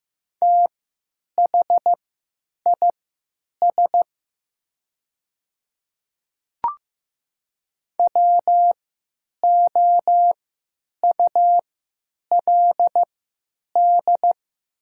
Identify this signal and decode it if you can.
Morse code, heard as a keyed tone, about 15 wpm, spaced as for 8 wpm, 700 Hz. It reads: THIS WOULD
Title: 3x Character Spacing - Top 100 Words - 15wpm